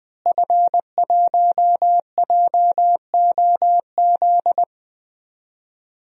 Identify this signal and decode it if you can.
Morse code, heard as a keyed tone, about 20 wpm, 700 Hz. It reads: F1JOZ
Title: DXCC-100 Call Signs - 5 Characters - Repeated 3x - 20wpm